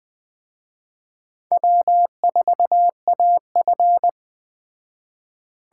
Morse code, heard as a keyed tone, about 21 wpm, 700 Hz. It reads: W4AF